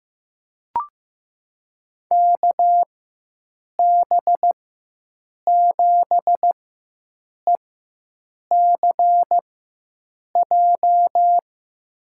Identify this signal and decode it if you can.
Morse code, heard as a keyed tone, about 15 wpm, spaced as for 7 wpm, 700 Hz. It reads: KB7ECJ
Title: USA Call Signs - 4x Character Spacing - 15wpm